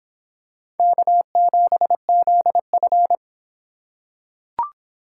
Morse code, heard as a keyed tone, about 26 wpm, 700 Hz. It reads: K7ZF